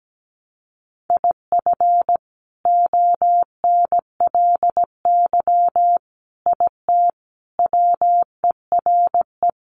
Morse code, heard as a keyed tone, about 17 wpm, 700 Hz. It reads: IF ONLY IT WERE
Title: Sentences from Top 500 Words 17wpm